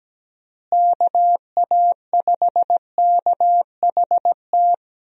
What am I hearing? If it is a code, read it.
Morse code, heard as a keyed tone, about 17 wpm, 700 Hz. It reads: KA5KHT
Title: USA Call Signs - 6 Characters - Repeated 3x - 17wpm